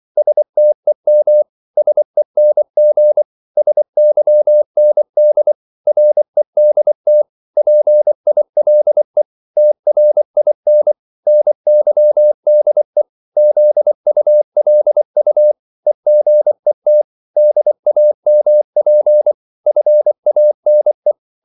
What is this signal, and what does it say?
Morse code, heard as a keyed tone, about 24 wpm, 600 Hz. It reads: STEM SENG SYND REDT PILE TRIN NYDE ZULU EGET DAMP FANE
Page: Korte danske ord 24wpm